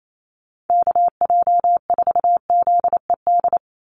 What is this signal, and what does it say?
Morse code, heard as a keyed tone, about 28 wpm, 700 Hz. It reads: KJ4ZED
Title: USA Call Signs 28wpm